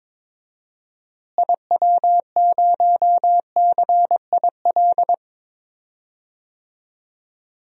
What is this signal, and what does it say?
Morse code, heard as a keyed tone, about 21 wpm, 700 Hz. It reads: IW0CIL
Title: DXCC-100 Call Signs - 6 Characters - Repeated 2x - 22wpm